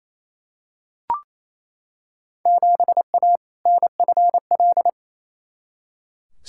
Morse code, heard as a keyed tone, about 28 wpm, 700 Hz. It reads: 7A NFL
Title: Field Day Exchanges Encore - 28wpm